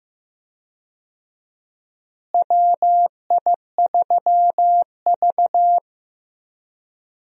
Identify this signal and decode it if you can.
Morse code, heard as a keyed tone, about 15 wpm, 700 Hz. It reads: WI3V
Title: USA Call Signs - 4 Characters - Repeated 2x - 15wpm